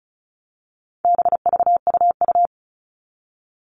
Morse code, heard as a keyed tone, about 34 wpm, 700 Hz. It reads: BVUU
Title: All Cut Numbers - 4-Digit 35wpm